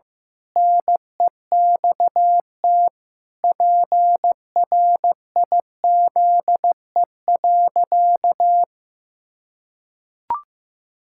Morse code, heard as a keyed tone, about 15 wpm, 700 Hz. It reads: NEXT PRIZE.
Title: Pangram Challenge v2 - 15wpm